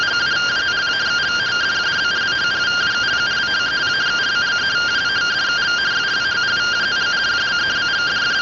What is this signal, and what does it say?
Two-tone teleprinter signal, about 170 Hz shift.